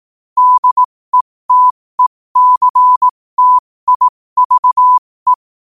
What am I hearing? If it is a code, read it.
Morse code, heard as a keyed tone, about 18 wpm, spaced as for 15 wpm, 1000 Hz. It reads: DETECTIVE
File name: qotdfiles / 15wpm